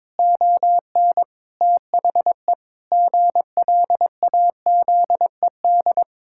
Morse code, heard as a keyed tone, about 22 wpm, 700 Hz. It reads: ON THE GLAZED
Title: Pangram Challenge v2 - 22wpm